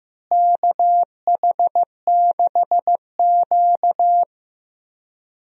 Morse code, heard as a keyed tone, about 15 wpm, 700 Hz. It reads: KH6Q
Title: DXCC-100 Call Signs - 4 Characters - Repeated 3x - 15wpm